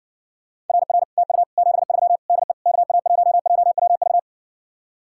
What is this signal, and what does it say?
Morse code, heard as a keyed tone, about 60 wpm, 700 Hz. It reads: RR TU 73 DE KA1JWV